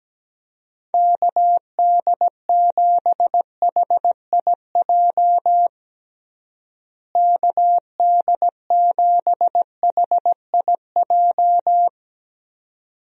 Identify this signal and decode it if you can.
Morse code, heard as a keyed tone, about 17 wpm, 700 Hz. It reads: KD7HIJ KD7HIJ
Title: USA Call Signs - 6 Characters - Repeated 2x - 17wpm